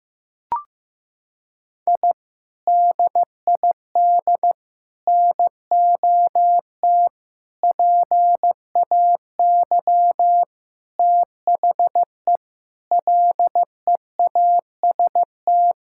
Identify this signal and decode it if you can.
Morse code, heard as a keyed tone, about 15 wpm, 700 Hz. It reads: I DID NOT PAY THE LEAST
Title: Sentences from Top 1000 Words - Review Part 6 - 15wpm